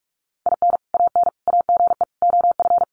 Morse code, heard as a keyed tone, about 44 wpm, 700 Hz. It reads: IN AN AGE OF
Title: Sentences from Top 1200 Words 45wpm